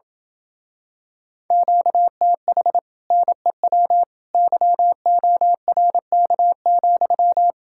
Morse code, heard as a keyed tone, about 27 wpm, 700 Hz. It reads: QTH NEW YORK,